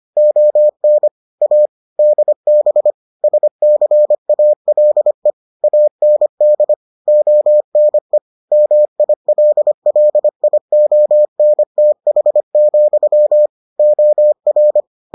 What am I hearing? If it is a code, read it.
Morse code, heard as a keyed tone, about 25 wpm, 600 Hz. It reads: ON A DB SCALE AND ONE MILLIONTH, OR